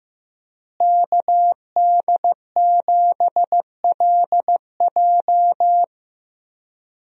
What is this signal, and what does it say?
Morse code, heard as a keyed tone, about 15 wpm, 700 Hz. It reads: KD7LJ
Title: USA Call Signs - 5 Characters - Repeated 3x - 15wpm